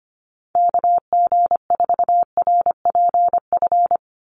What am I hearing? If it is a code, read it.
Morse code, heard as a keyed tone, about 25 wpm, 700 Hz. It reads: KG4RPF
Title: USA Call Signs 25wpm